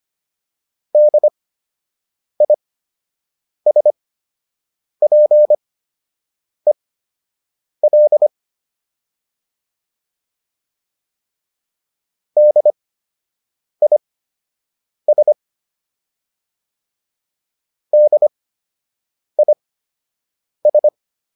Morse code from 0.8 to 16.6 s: DISPEL DIS

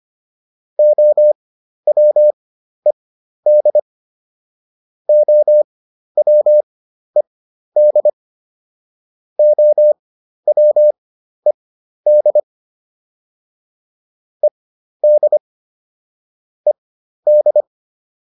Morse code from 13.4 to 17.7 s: ED ED

